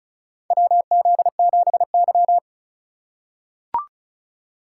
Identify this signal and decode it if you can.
Morse code, heard as a keyed tone, about 36 wpm, 700 Hz. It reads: WZ7Y
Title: USA Call Signs - 4x Character Spacing - 35wpm